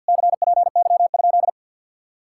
Morse code, heard as a keyed tone, about 58 wpm, 700 Hz. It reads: CPY?
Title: Speed-Racing Common QSO Phrases - 50wpm